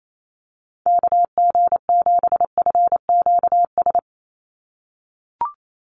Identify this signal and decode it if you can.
Morse code, heard as a keyed tone, about 28 wpm, 700 Hz. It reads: KG7FQS
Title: USA Call Signs 28wpm